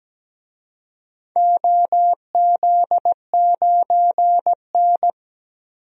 Morse code, heard as a keyed tone, about 17 wpm, 700 Hz. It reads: OZ9N
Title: DXCC-100 Call Signs - 4 Characters - 17wpm